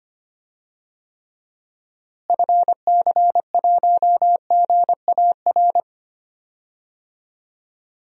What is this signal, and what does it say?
Morse code, heard as a keyed tone, about 25 wpm, 700 Hz. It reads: FC1GAR
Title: DXCC-100 Call Signs - 6 Characters - Repeated 3x - 25wpm